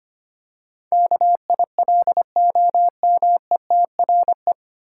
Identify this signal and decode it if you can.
Morse code, heard as a keyed tone, about 25 wpm, 700 Hz. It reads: KILOMETRE
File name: practice / 25wpm / 9 Letter Words - First 3 Letters Given - 25wpm